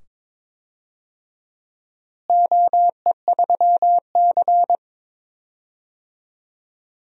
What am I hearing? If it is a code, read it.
Morse code, heard as a keyed tone, about 21 wpm, 700 Hz. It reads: OE3C
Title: DXCC-100 Call Signs - 4 Characters - Repeated 2x - 22wpm